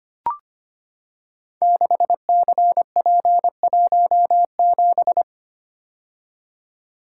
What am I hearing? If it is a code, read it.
Morse code, heard as a keyed tone, about 25 wpm, 700 Hz. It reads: 6CP17
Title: Code Groups - 5 Letter:Numbers - 25wpm